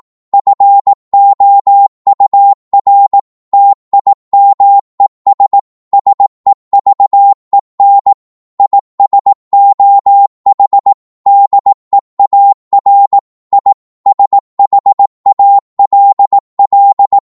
Morse code, heard as a keyed tone, about 18 wpm, 800 Hz. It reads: FOUR TIMES SEVEN IS OH DEAR I SHALL